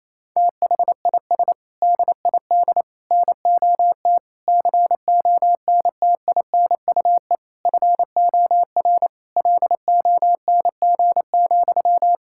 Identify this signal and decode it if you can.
Morse code, heard as a keyed tone, about 28 wpm, 700 Hz. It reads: THIS DID NOT CONTINUE FOR LONG,
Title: Sentences from Top 1000 Words - Review Part 6 - 28wpm